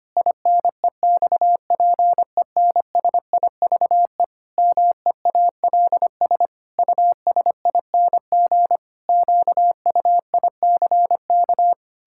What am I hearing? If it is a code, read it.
Morse code, heard as a keyed tone, about 25 wpm, 700 Hz. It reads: INEXPENSIVE MEALS USING QUICK
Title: Pangram Challenge v2 - 25wpm